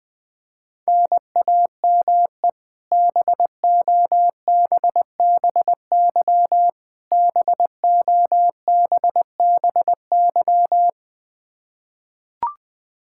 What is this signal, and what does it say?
Morse code, heard as a keyed tone, about 20 wpm, 700 Hz. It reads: NAME BOBBY BOBBY
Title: Practice QSO - Exchange 1 - Beginning - 20wpm